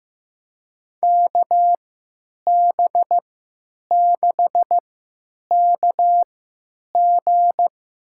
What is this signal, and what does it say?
Morse code, heard as a keyed tone, about 15 wpm, spaced as for 9 wpm, 700 Hz. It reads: KB6KG
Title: USA Call Signs - 3x Character Spacing - 15wpm